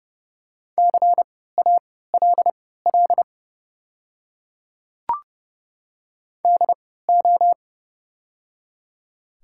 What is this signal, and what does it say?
Morse code, heard as a keyed tone, about 31 wpm, spaced as for 17 wpm, 700 Hz. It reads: CALL DO
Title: 3x Character Spacing - Top 100 Words - 30wpm